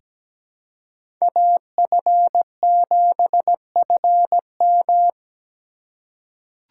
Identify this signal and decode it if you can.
Morse code, heard as a keyed tone, about 17 wpm, 700 Hz. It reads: AF7FM
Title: USA Call Signs - 4x Character Spacing - 17wpm